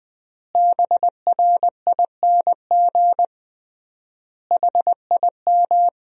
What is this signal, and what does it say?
Morse code, heard as a keyed tone, about 20 wpm, 700 Hz. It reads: BRING HIM
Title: 3x Word-Spacing - Sentences from Top 500 Words - 20wpm